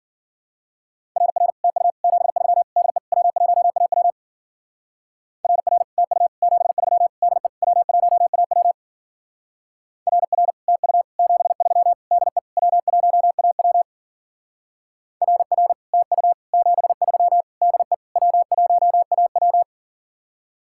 Morse code, heard as a keyed tone, about 50 wpm, 700 Hz. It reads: RR TU 73 DE W1AW RR TU 73 DE W1AW RR TU 73 DE W1AW RR TU 73 DE W1AW
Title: Speed-Racing Common QSO Phrases - 40wpm